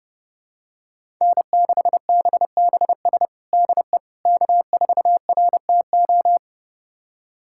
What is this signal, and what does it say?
Morse code, heard as a keyed tone, about 30 wpm, 700 Hz. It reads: N6BBS DE K4RTO
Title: USA Callsign DE Callsign - 5 Characters - 30wpm